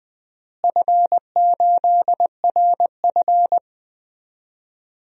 Morse code, heard as a keyed tone, about 20 wpm, 700 Hz. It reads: F8RF
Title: DXCC-100 Call Signs - 4 Characters - Repeated 2x - 20wpm